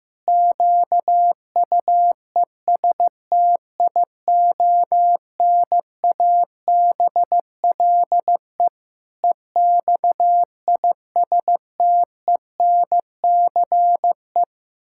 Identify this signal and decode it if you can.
Morse code, heard as a keyed tone, about 15 wpm, 700 Hz. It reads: QUESTIONABLE EXISTENCE
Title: Pangram Challenge v2 - 15wpm